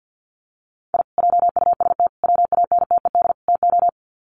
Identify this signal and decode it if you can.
Morse code, heard as a keyed tone, about 48 wpm, 700 Hz. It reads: I JUST WANTED TO
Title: Sets of 4 Words 50wpm